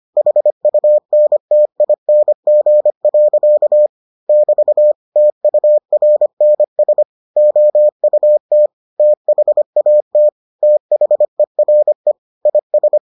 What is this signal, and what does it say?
Morse code, heard as a keyed tone, about 25 wpm, 600 Hz. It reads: HUNTING. <BT> TURNS OUT THAT THERE IS